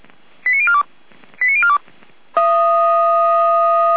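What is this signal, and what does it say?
Five-tone selective call (ZVEI), heard as quick runs of five tones, tones of about 70 ms. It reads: 89042 89042
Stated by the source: ZVEI five-tone selective call 89140 received on SDR, from the feuerwehr-orlamuende fire brigade audio server.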